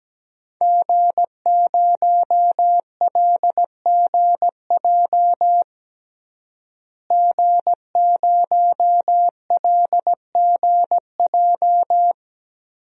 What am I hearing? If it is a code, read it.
Morse code, heard as a keyed tone, about 17 wpm, 700 Hz. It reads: G0LGJ G0LGJ
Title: DXCC-100 Call Signs - 5 Characters - Repeated 2x - 17wpm